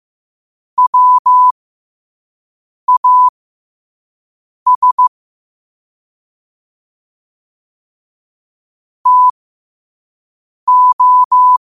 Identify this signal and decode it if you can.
Morse code, heard as a keyed tone, about 15 wpm, spaced as for 5 wpm, 1000 Hz. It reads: WAS TO
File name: qotdfiles / 5wpm